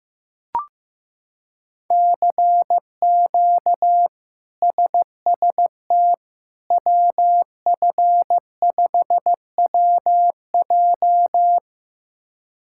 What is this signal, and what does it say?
Morse code, heard as a keyed tone, about 15 wpm, 700 Hz. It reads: CQ SST WF5WJ
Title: K1USN SST QSO Practice - 15wpm